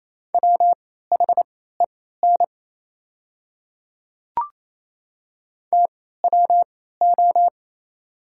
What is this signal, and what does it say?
Morse code, heard as a keyed tone, about 28 wpm, spaced as for 16 wpm, 700 Hz. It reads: WHEN TWO
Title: 3x Character Spacing - Top 100 Words - 28wpm